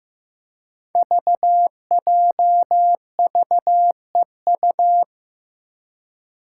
Morse code, heard as a keyed tone, about 15 wpm, 700 Hz. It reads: VJVEU
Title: Five Challenging Letter-Number Combinations 15wpm